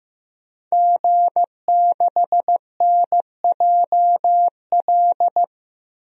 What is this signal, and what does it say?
Morse code, heard as a keyed tone, about 15 wpm, 700 Hz. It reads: G6NJL
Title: DXCC-100 Call Signs - 5 Characters - 15wpm